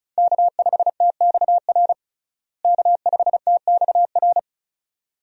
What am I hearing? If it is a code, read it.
Morse code, heard as a keyed tone, about 36 wpm, 700 Hz. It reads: K5TXR K5TXR
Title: USA Call Signs - 5 Characters - Repeated 2x - 35wpm